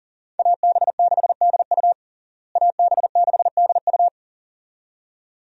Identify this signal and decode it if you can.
Morse code, heard as a keyed tone, about 40 wpm, 700 Hz. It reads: AB6DU AB6DU
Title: USA Call Signs - 5 Characters - Repeated 2x - 40wpm